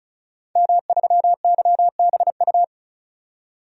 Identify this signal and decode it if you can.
Morse code, heard as a keyed tone, about 36 wpm, 700 Hz. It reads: M3YBU
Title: DXCC-100 Call Signs - 5 Characters - 35wpm